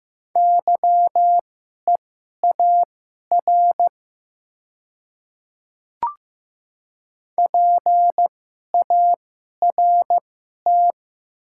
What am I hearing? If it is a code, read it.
Morse code, heard as a keyed tone, about 15 wpm, spaced as for 11 wpm, 700 Hz. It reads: YEAR PART
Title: 2x Character Spacing - Top 200 Words - 15wpm